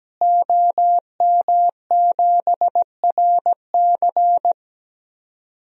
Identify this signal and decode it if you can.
Morse code, heard as a keyed tone, about 17 wpm, 700 Hz. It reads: OM7RC
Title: DXCC-100 Call Signs - 5 Characters - Repeated 2x - 17wpm